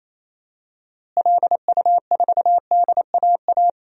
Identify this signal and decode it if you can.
Morse code, heard as a keyed tone, about 28 wpm, 700 Hz. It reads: LU4DAA
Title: DXCC-100 Call Signs - 6 Characters - Repeated 3x - 28wpm